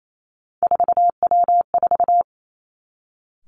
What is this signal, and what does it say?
Morse code, heard as a keyed tone, about 28 wpm, 700 Hz. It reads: 4W4
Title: Three Challenging Letter-Number Combinations 28wpm